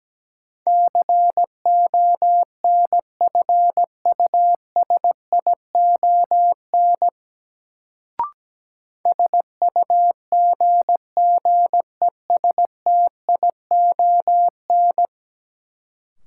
Morse code, heard as a keyed tone, about 17 wpm, 700 Hz. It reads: CONFUSION SUGGESTION
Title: Sentences from Top 1600 Words - New Word Drill - 17wpm